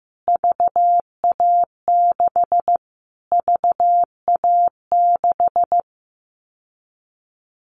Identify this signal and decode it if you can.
Morse code, heard as a keyed tone, about 15 wpm, 700 Hz. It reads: VA6 VA6
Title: Callsign Prefixes for Top 25 DXCC - Repeated 3x - 15wpm